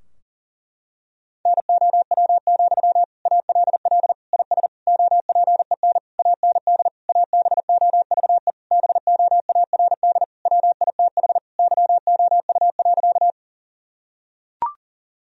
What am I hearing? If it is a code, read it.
Morse code, heard as a keyed tone, about 37 wpm, 700 Hz. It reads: NOW, ALL IS OPEN AND ABOVE BOARD WITH YOU.
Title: Sentences from Top 1000 Words - Review Part 4 - 40wpm